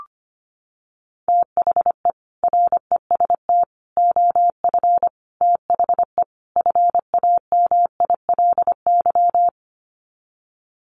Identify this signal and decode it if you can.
Morse code, heard as a keyed tone, about 25 wpm, 700 Hz. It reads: THE REST OF THE FAMILY
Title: Sets of 5 Words Encore2 25wpm